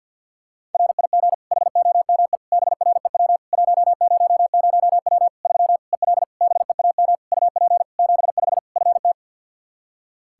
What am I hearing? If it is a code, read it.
Morse code, heard as a keyed tone, about 51 wpm, 700 Hz. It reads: RIG HOME BREW 100W 3 EL BEAM UP 75 FT
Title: K1USN SST QSO Practice 50wpm